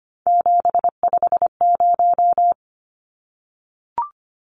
Morse code, heard as a keyed tone, about 25 wpm, 700 Hz. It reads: 750
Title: Numbers - 3-Digit 25wpm